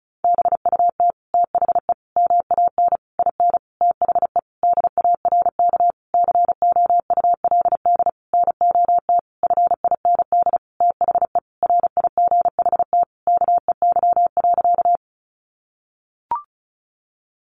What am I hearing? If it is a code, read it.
Morse code, heard as a keyed tone, about 34 wpm, 700 Hz. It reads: BUT THE MAN IN THE DARK COULD NOT FIND THE RIGHT KEY.
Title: Sentences from Top 600 Words 35wpm